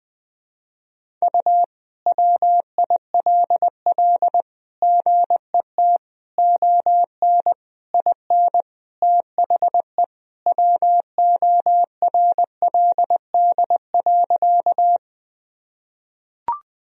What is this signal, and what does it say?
Morse code, heard as a keyed tone, about 20 wpm, 700 Hz. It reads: U WILL GET ON IN THE WORLD.
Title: Sentences from Top 200 Words and Abbreviations 20wpm